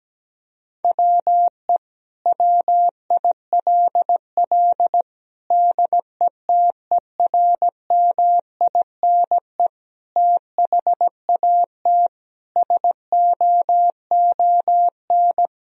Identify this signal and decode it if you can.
Morse code, heard as a keyed tone, about 17 wpm, 700 Hz. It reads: WE WILL DETERMINE THAT SOON
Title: Sentences from Top 1000 Words - Review Part 4 - 17wpm